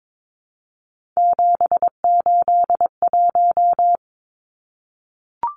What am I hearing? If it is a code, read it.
Morse code, heard as a keyed tone, about 22 wpm, 700 Hz. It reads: 781
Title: Numbers - 3-Digit 22wpm